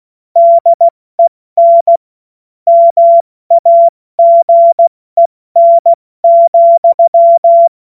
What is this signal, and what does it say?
Morse code, heard as a keyed tone, about 16 wpm, 680 Hz. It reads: DEN MAGEN,
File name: Morsekurs / Witze / Witz 08. Februar 2022 70 F80